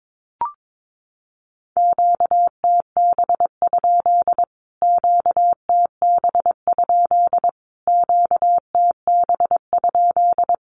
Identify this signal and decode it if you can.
Morse code, heard as a keyed tone, about 22 wpm, 700 Hz. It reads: QTB? QTB? QTB?
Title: Q-Codes Repeated 3x 22wpm